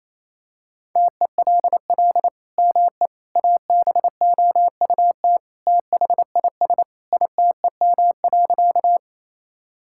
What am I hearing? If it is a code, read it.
Morse code, heard as a keyed tone, about 28 wpm, 700 Hz. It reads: TELL ME ABOUT THIS ITEM.
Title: Sentences from Top 1000 Words - Review Part 6 - 28wpm